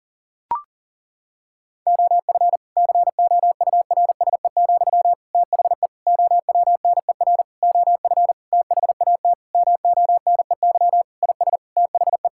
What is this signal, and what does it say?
Morse code, heard as a keyed tone, about 37 wpm, 700 Hz. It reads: OF COURSE, THE OWNER OF THAT MONEY IS THE
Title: Sentences from Top 1600 Words 40wpm